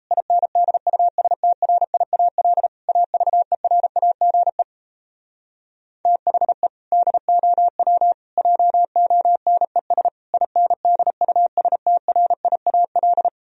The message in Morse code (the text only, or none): INDUSTRIAL AVERAGE THE DOW JONES INDUSTRIAL